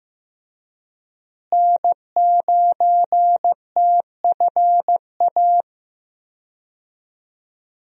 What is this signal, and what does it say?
Morse code, heard as a keyed tone, about 15 wpm, 700 Hz. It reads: N9TFA
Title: USA Call Signs - 5 Characters - Repeated 3x - 15wpm